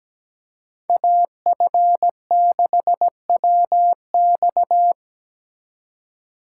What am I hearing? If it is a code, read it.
Morse code, heard as a keyed tone, about 19 wpm, 700 Hz. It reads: AF6WX
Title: Speed Racing - USA 2x2 Call Signs 17wpm